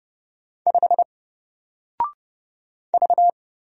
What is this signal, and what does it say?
Morse code, heard as a keyed tone, about 30 wpm, 700 Hz. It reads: 5 V
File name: Single-Letter-Number-30wpm